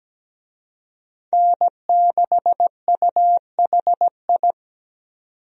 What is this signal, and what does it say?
Morse code, heard as a keyed tone, about 17 wpm, 700 Hz. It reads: N6UHI
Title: USA Call Signs - 5 Characters - 17wpm